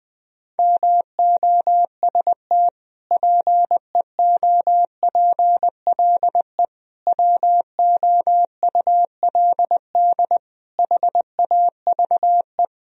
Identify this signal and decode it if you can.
Morse code, heard as a keyed tone, about 20 wpm, 700 Hz. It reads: MOST PEOPLE WOULD HAVE
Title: Sentences from Top 1000 Words - Review Part 1 - 20wpm